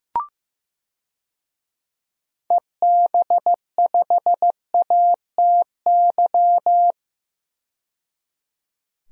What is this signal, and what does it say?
Morse code, heard as a keyed tone, about 16 wpm, 700 Hz. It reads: EB5ATY
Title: DXCC-100 Call Signs - 6 Characters - 15wpm